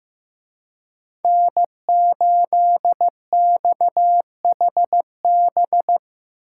Morse code, heard as a keyed tone, about 18 wpm, 700 Hz. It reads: N8XHB
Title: Speed Racing - USA 1x3 Call Signs 15wpm